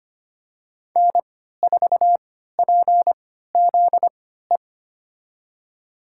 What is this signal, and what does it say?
Morse code, heard as a keyed tone, about 25 wpm, spaced as for 14 wpm, 700 Hz. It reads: N4PZE
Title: USA Call Signs - 5 Characters - 3x Character Spacing - 25wpm